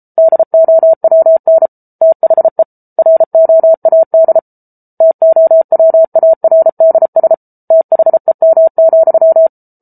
Morse code, heard as a keyed tone, about 33 wpm, 650 Hz. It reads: DOWN THE ROAD TOWARDS THEM,